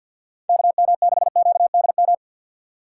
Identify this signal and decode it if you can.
Morse code, heard as a keyed tone, about 53 wpm, 700 Hz. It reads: KM6QDM
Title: USA Call Signs - 4x Character Spacing - 50wpm